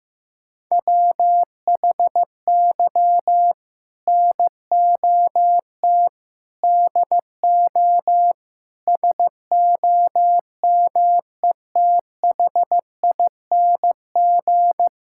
Morse code, heard as a keyed tone, about 16 wpm, 700 Hz. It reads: WHY NOT DO SOMETHING
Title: Sentences from Top 1000 Words - Review Part 6 - 15wpm